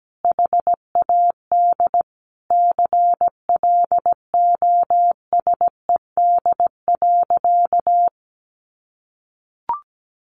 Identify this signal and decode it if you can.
Morse code, heard as a keyed tone, about 17 wpm, 700 Hz. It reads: HAD CLOSED.
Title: Sentences from Top 1400 Words 17wpm